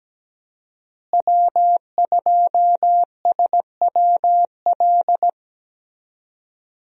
Morse code, heard as a keyed tone, about 17 wpm, 700 Hz. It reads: W2SWL
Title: USA Call Signs - 5 Characters - Repeated 2x - 17wpm